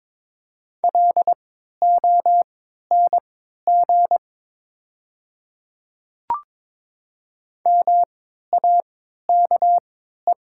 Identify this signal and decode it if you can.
Morse code, heard as a keyed tone, about 22 wpm, spaced as for 13 wpm, 700 Hz. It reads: LONG MAKE
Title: 3x Character Spacing - Top 100 Words - 22wpm